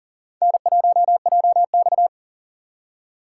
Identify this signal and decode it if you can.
Morse code, heard as a keyed tone, about 40 wpm, 700 Hz. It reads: N1JX
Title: USA Call Signs - 4 Characters - 4x Character Spacing - 40wpm